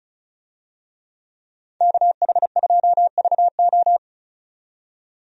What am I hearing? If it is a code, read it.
Morse code, heard as a keyed tone, about 34 wpm, 700 Hz. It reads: KH2VO
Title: USA Call Signs - 5 Characters - 35wpm